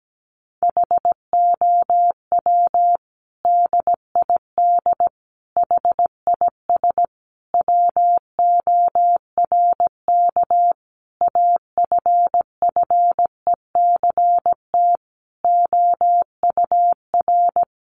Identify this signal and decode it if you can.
Morse code, heard as a keyed tone, about 17 wpm, 700 Hz. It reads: HOW DID HIS WORK AFFECT OUR
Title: Sentences from Top 900 Words 17wpm